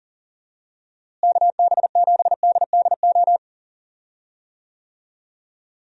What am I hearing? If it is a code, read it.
Morse code, heard as a keyed tone, about 40 wpm, 700 Hz. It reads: KB7DDO
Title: USA Call Signs - 6 Characters - Repeated 2x - 40wpm